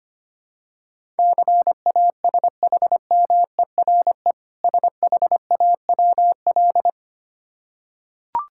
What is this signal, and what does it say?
Morse code, heard as a keyed tone, about 25 wpm, 700 Hz. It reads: CASHMERE SHAWL
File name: practice / 25wpm / Difficult Words with Hints - 25wpm